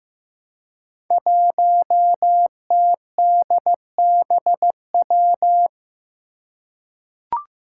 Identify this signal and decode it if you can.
Morse code, heard as a keyed tone, about 15 wpm, 700 Hz. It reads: 1TDBW
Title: Five Challenging Letter-Number Combinations 15wpm